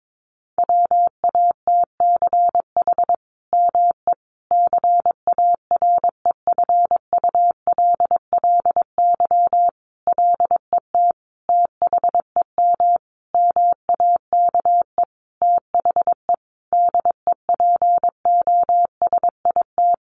Morse code, heard as a keyed tone, about 22 wpm, 700 Hz. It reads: WATCH ME CAREFULLY LET THEM MAKE THE DEPOSIT
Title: Sentences from Top 1300 Words 22wpm